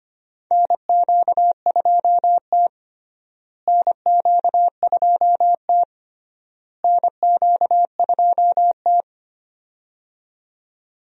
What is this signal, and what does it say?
Morse code, heard as a keyed tone, about 25 wpm, 700 Hz. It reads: NQ2T NQ2T NQ2T
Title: USA Call Signs - 4 Characters - Repeated 3x - 25wpm